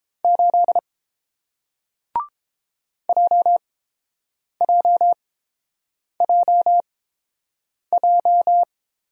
Morse code, heard as a keyed tone, about 30 wpm, spaced as for 7 wpm, 700 Hz. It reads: 8 JJJJ